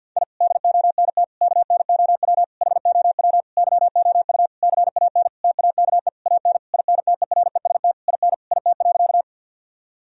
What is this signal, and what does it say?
Morse code, heard as a keyed tone, about 51 wpm, 700 Hz. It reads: I DONT KNOW HOW YOU CAN TAKE AN INTEREST IN IT.
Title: Sentences from Top 1000 Words - Review Part 2 - 50wpm